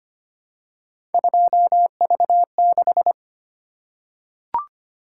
Morse code, heard as a keyed tone, about 25 wpm, 700 Hz. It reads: 2V6
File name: Three Letter-Number Combinations - 25wpm